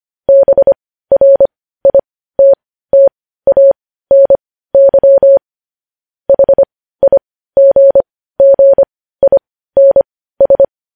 Morse code, heard as a keyed tone, about 25 wpm, spaced as for 15 wpm, 550 Hz. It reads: BRITTANY HIGGINS